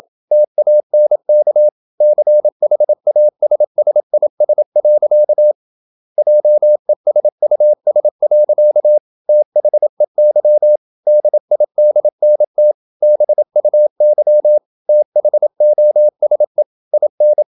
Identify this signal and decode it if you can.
Morse code, heard as a keyed tone, about 27 wpm, 600 Hz. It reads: TANK CHASSIS. JESUS. THEY DIDNT BUY THOSE IN
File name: Never 0027wpm